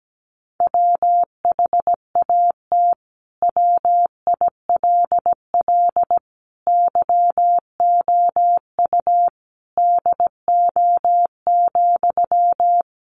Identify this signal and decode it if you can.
Morse code, heard as a keyed tone, about 17 wpm, 700 Hz. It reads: WHAT WILL YOU DO,
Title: Sentences from Top 100 Words 17wpm